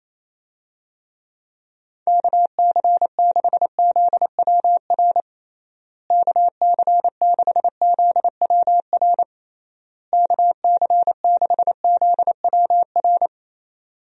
Morse code, heard as a keyed tone, about 28 wpm, 700 Hz. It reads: KC6ZWR KC6ZWR KC6ZWR
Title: USA Call Signs - 6 Characters - Repeated 3x - 28wpm